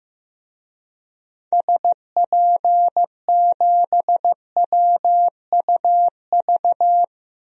Morse code, heard as a keyed tone, about 15 wpm, 700 Hz. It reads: SP7WUV
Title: DXCC-100 Call Signs - 6 Characters - 15wpm